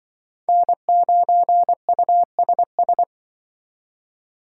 Morse code, heard as a keyed tone, about 25 wpm, 700 Hz. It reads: N9USS
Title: Speed Racing - USA Call Signs 20wpm